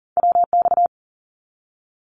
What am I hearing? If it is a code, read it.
Morse code, heard as a keyed tone, about 38 wpm, 700 Hz. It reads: WX
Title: Abbreviations 40wpm